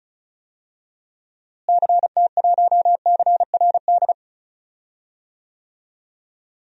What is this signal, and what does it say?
Morse code, heard as a keyed tone, about 34 wpm, 700 Hz. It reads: CT1CRD
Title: DXCC-100 Call Signs - 6 Characters - Repeated 2x - 35wpm